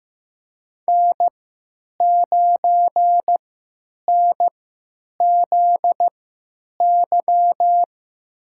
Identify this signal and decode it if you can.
Morse code, heard as a keyed tone, about 15 wpm, spaced as for 9 wpm, 700 Hz. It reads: N9NZY
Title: USA Call Signs - 5 Characters - 3x Character Spacing - 15wpm